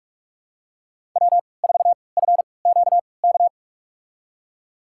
Morse code, heard as a keyed tone, about 44 wpm, spaced as for 25 wpm, 700 Hz. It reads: W4FQK
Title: USA Call Signs - 5 Characters - 3x Character Spacing - 45wpm